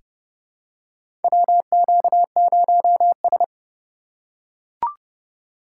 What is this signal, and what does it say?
Morse code, heard as a keyed tone, about 30 wpm, 700 Hz. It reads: WQ0S
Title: USA Call Signs - 4 Characters - 2x Character Spacing - 30wpm